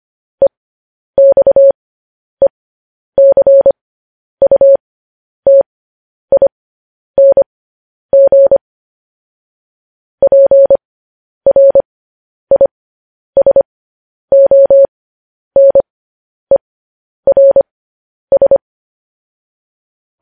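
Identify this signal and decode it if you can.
Morse code, heard as a keyed tone, about 25 wpm, spaced as for 10 wpm, 550 Hz. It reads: EXECUTING PRISONERS
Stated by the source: News headlines sent in Morse at 10 wpm.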